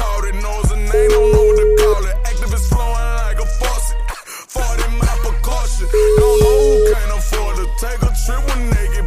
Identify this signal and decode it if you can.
Ringback tone (Europe, 425 Hz), with noise.